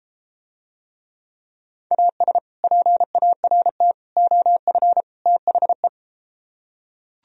Morse code, heard as a keyed tone, about 33 wpm, 700 Hz. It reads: AS PART OF THE